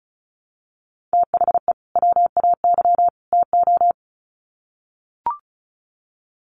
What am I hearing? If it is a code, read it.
Morse code, heard as a keyed tone, about 34 wpm, 700 Hz. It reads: THE WAY TO
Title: Sets of 3 Words 35wpm